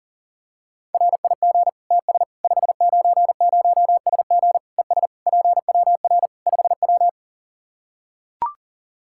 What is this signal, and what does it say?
Morse code, heard as a keyed tone, about 37 wpm, 700 Hz. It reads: RIG TS 590SG ES PWR 5W
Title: Practice QSO - Exchange 2 - Info - 40wpm